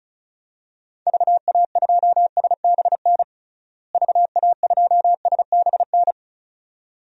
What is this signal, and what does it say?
Morse code, heard as a keyed tone, about 34 wpm, 700 Hz. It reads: VA2SBN VA2SBN
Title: DXCC-100 Call Signs - 6 Characters - Repeated 2x - 35wpm